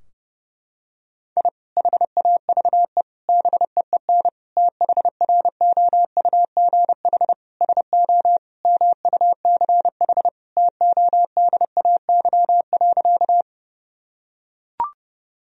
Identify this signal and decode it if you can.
Morse code, heard as a keyed tone, about 30 wpm, 700 Hz. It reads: I HAVE BEEN THROUGH SO MUCH TODAY.
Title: Sentences from Top 1000 Words - Review Part 4 - 30wpm